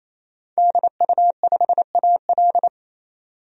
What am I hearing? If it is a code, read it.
Morse code, heard as a keyed tone, about 28 wpm, 700 Hz. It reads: DU5AL
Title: DXCC-100 Call Signs - 5 Characters - 28wpm